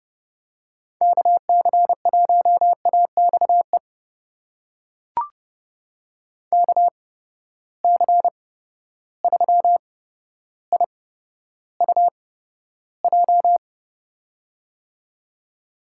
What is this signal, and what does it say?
Morse code, heard as a keyed tone, about 30 wpm, spaced as for 13 wpm, 700 Hz. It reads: KC1AXE KC3IUJ
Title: USA Call Signs - 6 Characters - 8x Character Spacing - 30wpm